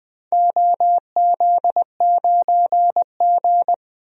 Morse code, heard as a keyed tone, about 21 wpm, 700 Hz. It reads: OZ9G
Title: DXCC-100 Call Signs - 4 Characters - 20wpm